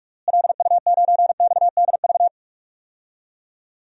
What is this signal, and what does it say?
Morse code, heard as a keyed tone, about 42 wpm, 700 Hz. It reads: LU9XDV
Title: DXCC-100 Call Signs - 6 Characters - Repeated 3x - 45wpm